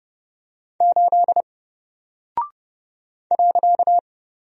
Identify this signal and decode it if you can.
Morse code, heard as a keyed tone, about 31 wpm, 700 Hz. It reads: 8 .
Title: Single Character (Letter:Number:Punctuation:Prosign) 30wpm